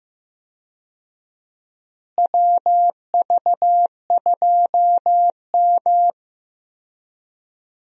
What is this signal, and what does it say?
Morse code, heard as a keyed tone, about 15 wpm, 700 Hz. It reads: WV2M
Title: USA Call Signs - 4 Characters - Repeated 2x - 15wpm